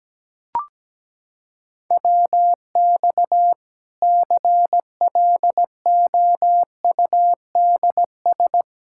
Morse code, heard as a keyed tone, about 17 wpm, 700 Hz. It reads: WX CLOUDS
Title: K1USN SST QSO Practice 17wpm